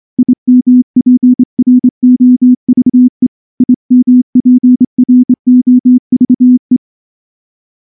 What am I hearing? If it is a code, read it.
Morse code, heard as a keyed tone, about 24 wpm, 250 Hz. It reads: IMPROVE IMPROVE
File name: morse / morse-training-25WPM